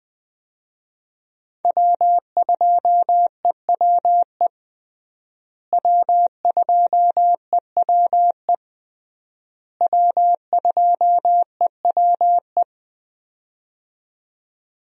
Morse code, heard as a keyed tone, about 21 wpm, 700 Hz. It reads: W2EWE W2EWE W2EWE
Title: USA Call Signs - 5 Characters - Repeated 3x - 20wpm